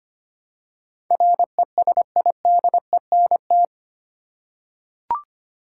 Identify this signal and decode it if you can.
Morse code, heard as a keyed tone, about 25 wpm, 700 Hz. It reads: RESIDENT
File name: Eight Letter Words - 25wpm